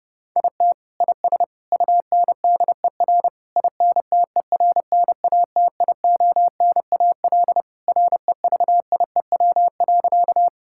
Morse code, heard as a keyed tone, about 32 wpm, 700 Hz. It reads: IT IS UNDER INTERNATIONAL REVIEW.
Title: Sentences from Top 1000 Words - Review Part 6 - 30wpm